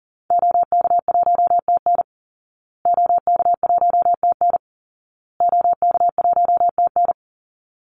Morse code, heard as a keyed tone, about 40 wpm, 700 Hz. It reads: OK1TN OK1TN OK1TN
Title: Call Signs Repeated 2x - 40wpm